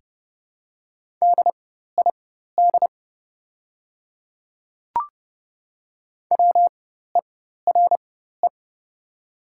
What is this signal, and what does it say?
Morse code, heard as a keyed tone, about 30 wpm, spaced as for 14 wpm, 700 Hz. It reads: DID WERE
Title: 4x Character Spacing - Top 100 Words - 30wpm